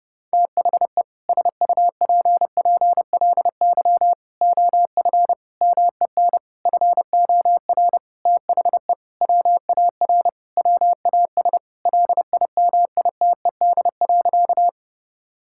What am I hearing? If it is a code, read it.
Morse code, heard as a keyed tone, about 30 wpm, 700 Hz. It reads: THE SUPPLY OF MEN FOR THE WAR WAS LIMITED.
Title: Sentences from Top 1000 Words - Review Part 6 - 30wpm